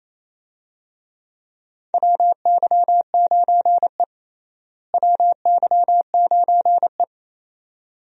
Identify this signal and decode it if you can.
Morse code, heard as a keyed tone, about 27 wpm, 700 Hz. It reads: WY9E WY9E
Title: USA Call Signs - 4 Characters - Repeated 2x - 28wpm